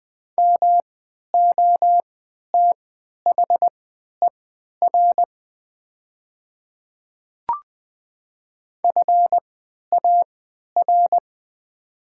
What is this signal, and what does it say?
Morse code, heard as a keyed tone, about 20 wpm, spaced as for 11 wpm, 700 Hz. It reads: MOTHER FAR
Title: 3x Character Spacing - Top 300 Words - 20wpm